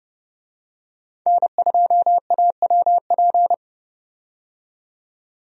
Morse code, heard as a keyed tone, about 29 wpm, 700 Hz. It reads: N2AWP
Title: USA Call Signs - 5 Characters - 30wpm